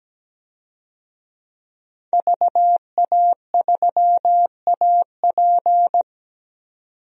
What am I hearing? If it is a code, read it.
Morse code, heard as a keyed tone, about 17 wpm, 700 Hz. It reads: VA3AP
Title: DXCC-100 Call Signs - 5 Characters - 17wpm